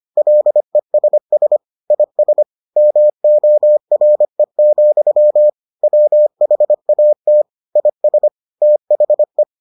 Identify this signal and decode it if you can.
Morse code, heard as a keyed tone, about 25 wpm, 600 Hz. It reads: LESS IS MORE, WHAT IS THE